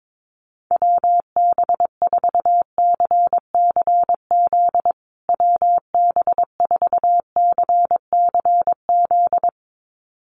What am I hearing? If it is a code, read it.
Morse code, heard as a keyed tone, about 22 wpm, 700 Hz. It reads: WB4CCZ WB4CCZ
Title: USA Call Signs Repeated 2x 22wpm